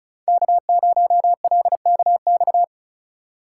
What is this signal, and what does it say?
Morse code, heard as a keyed tone, about 34 wpm, 700 Hz. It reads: K0LKX
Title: USA Call Signs - 5 Characters - 3x Character Spacing - 35wpm